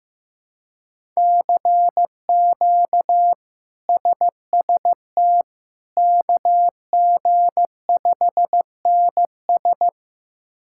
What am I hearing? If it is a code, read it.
Morse code, heard as a keyed tone, about 15 wpm, 700 Hz. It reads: CQ SST KG5NS
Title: K1USN SST QSO Practice - 15wpm